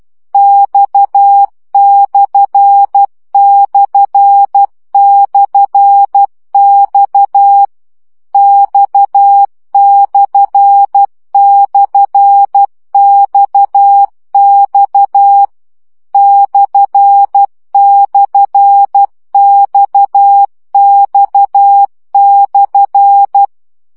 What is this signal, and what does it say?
Morse code, heard as a keyed tone, about 12 wpm, 800 Hz. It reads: X///X X//XX //XX/
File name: CW gyakorlas x per 10 baud